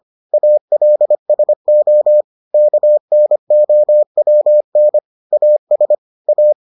Morse code, heard as a keyed tone, about 25 wpm, 600 Hz. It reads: ALSO KNOWN AS A